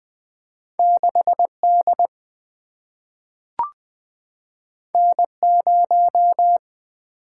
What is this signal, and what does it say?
Morse code, heard as a keyed tone, about 20 wpm, 700 Hz. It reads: 6D N0
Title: Single-Letter Single-Number 20wpm